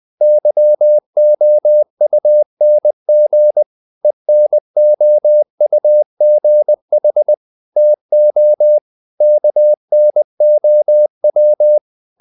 Morse code, heard as a keyed tone, about 20 wpm, 600 Hz. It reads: YOUNG ENOUGH TO KNOW
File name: CW / Practice / EnglishQuotations0320